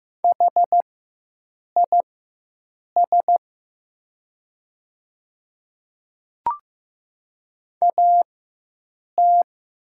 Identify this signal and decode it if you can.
Morse code, heard as a keyed tone, about 15 wpm, spaced as for 7 wpm, 700 Hz. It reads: HIS AT
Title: 4x Character Spacing - Top 100 Words - 15wpm